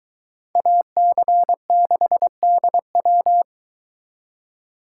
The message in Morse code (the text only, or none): AC6DW